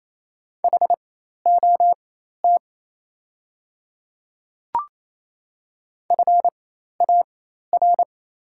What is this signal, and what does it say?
Morse code, heard as a keyed tone, about 28 wpm, spaced as for 13 wpm, 690 Hz. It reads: HOT FAR
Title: 4x Character Spacing - Top 300 Words - 28wpm